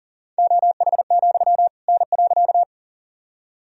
Morse code, heard as a keyed tone, about 40 wpm, 700 Hz. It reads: OH, N.
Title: Sentences from Top 500 Words and Abbreviations 40wpm